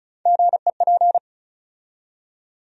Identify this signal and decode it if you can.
Morse code, heard as a keyed tone, about 34 wpm, 700 Hz. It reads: GEP
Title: USA Call Sign Postfixes - 3 Letters 35wpm